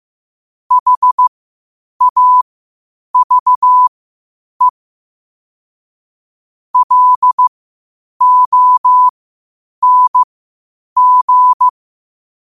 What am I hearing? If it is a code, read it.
Morse code, heard as a keyed tone, about 15 wpm, spaced as for 8 wpm, 1000 Hz. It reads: HAVE LONG